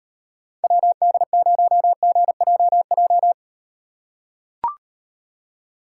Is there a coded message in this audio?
Yes: Morse code from 0.6 to 3.3 s, heard as a keyed tone, about 38 wpm, 700 Hz. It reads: WD0GJJ